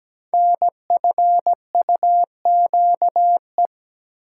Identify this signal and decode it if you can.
Morse code, heard as a keyed tone, about 17 wpm, 700 Hz. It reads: NFUQE
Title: Code Groups - 5 Letter:Numbers - 17wpm